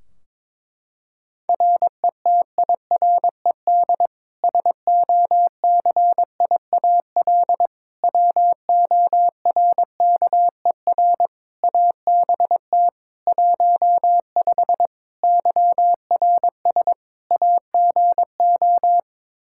Morse code, heard as a keyed tone, about 21 wpm, 700 Hz. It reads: RETIRED SOCIAL WORKER ABT 15 YRS AGO
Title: Practice QSO - Exchange 3 - More Info - 22wpm